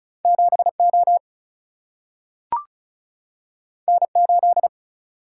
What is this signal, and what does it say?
Morse code, heard as a keyed tone, about 36 wpm, 700 Hz. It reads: 7 O N 8
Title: Single-Letter Single-Number 35wpm